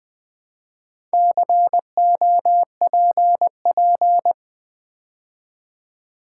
Morse code, heard as a keyed tone, about 20 wpm, 690 Hz. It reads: COPP
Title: Build A Word - 2 Letters At A Time - 6-Letter Words - 20wpm